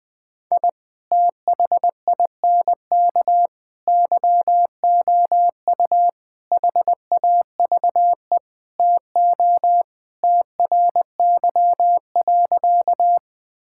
Morse code, heard as a keyed tone, about 21 wpm, 700 Hz. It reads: I THINK YOU HAVE TO TRY.
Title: Sentences with 5-Word Phrases - 20wpm